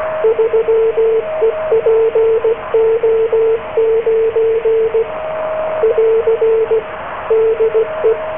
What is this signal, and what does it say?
Morse code, heard as a keyed tone, about 16 wpm, 450 Hz, with radio noise. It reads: 3EPO9 <AR> DE